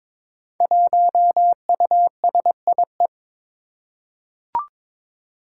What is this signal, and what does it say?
Morse code, heard as a keyed tone, about 22 wpm, 700 Hz. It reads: 1USIE
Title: Five Challenging Letter-Number Combinations 22wpm